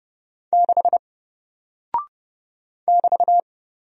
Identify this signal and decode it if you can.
Morse code, heard as a keyed tone, about 30 wpm, 700 Hz. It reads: 6 <BT>
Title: Single Character (Letter:Number:Punctuation:Prosign) Alt 30wpm